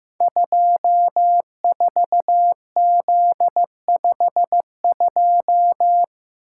Morse code, heard as a keyed tone, about 15 wpm, 700 Hz. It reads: 24Z52
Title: Code Groups - 5 Letter:Numbers - 15wpm